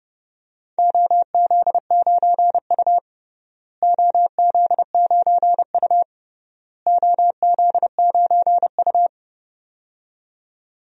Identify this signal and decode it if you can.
Morse code, heard as a keyed tone, about 32 wpm, 700 Hz. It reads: OZ9U OZ9U OZ9U
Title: DXCC-100 Call Signs - 4 Characters - Repeated 3x - 30wpm